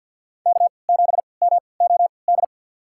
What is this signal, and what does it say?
Morse code, heard as a keyed tone, about 51 wpm, spaced as for 28 wpm, 700 Hz. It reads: K7MOD
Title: USA Call Signs - 5 Characters - 3x Character Spacing - 50wpm